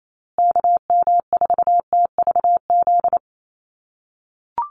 Morse code, heard as a keyed tone, about 28 wpm, 700 Hz. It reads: KM4TVZ
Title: USA Call Signs Repeated 3x 28wpm